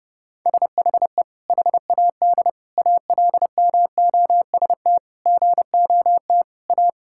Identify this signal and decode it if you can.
Morse code, heard as a keyed tone, about 30 wpm, 700 Hz. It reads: SHE HAD ALMOST GOT A
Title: Sentences from Top 1000 Words - Review Part 5 - 30wpm